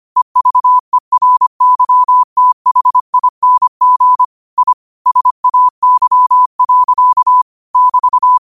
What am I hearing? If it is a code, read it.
Morse code, heard as a keyed tone, about 25 wpm, 1000 Hz. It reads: EVERYTHING I SAY. <BT>
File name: qotdfiles / 25wpm